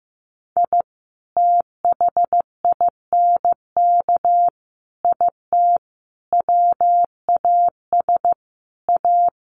Morse code, heard as a keyed tone, about 15 wpm, 700 Hz. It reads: I THINK IT WAS A
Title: Sets of 5 Words 15wpm